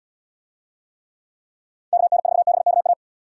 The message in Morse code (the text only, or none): KN4BKA